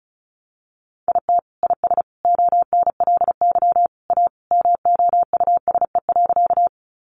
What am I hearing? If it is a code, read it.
Morse code, heard as a keyed tone, about 34 wpm, 700 Hz. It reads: IT IS ONLY A MOUSE.
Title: Sentences from Top 700 Words 35wpm